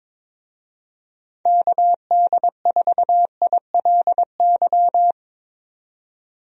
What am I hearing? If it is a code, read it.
Morse code, heard as a keyed tone, about 22 wpm, 700 Hz. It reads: KD4ILY
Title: USA Call Signs - 6 Characters - 22wpm